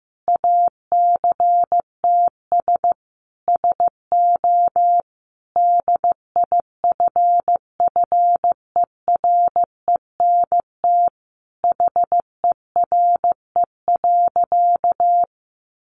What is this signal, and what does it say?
Morse code, heard as a keyed tone, about 15 wpm, 700 Hz. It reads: ACTS SO DIFFERENT HERE.
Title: Easy Phrases 15wpm